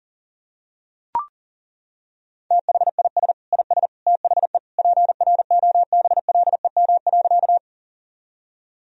Morse code, heard as a keyed tone, about 40 wpm, 700 Hz. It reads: THIS IS THE PROBLEM.
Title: Sentences from Top 1000 Words - Review Part 2 - 40wpm